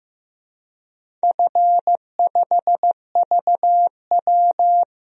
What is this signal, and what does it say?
Morse code, heard as a keyed tone, about 15 wpm, 700 Hz. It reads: F5VW
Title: DXCC-100 Call Signs - 4 Characters - 15wpm